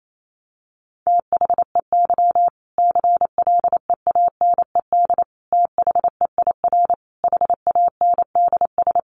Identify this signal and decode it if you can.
Morse code, heard as a keyed tone, about 28 wpm, 700 Hz. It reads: THEY CLEANED THEIR HANDS
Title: A Sound Of Thunder 28wpm